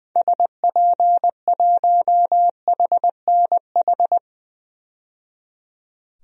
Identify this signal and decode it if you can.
Morse code, heard as a keyed tone, about 20 wpm, 700 Hz. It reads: SP1HNH
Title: DXCC-100 Call Signs - 6 Characters - 20wpm